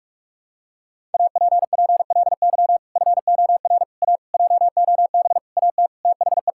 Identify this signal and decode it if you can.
Morse code, heard as a keyed tone, about 42 wpm, 700 Hz. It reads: APPLY FOR A JOB AT THE
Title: Sentences from Top 1600 Words 45wpm